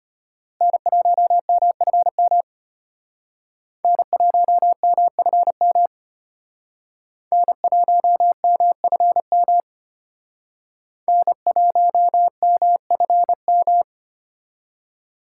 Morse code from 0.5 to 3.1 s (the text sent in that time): N1MFM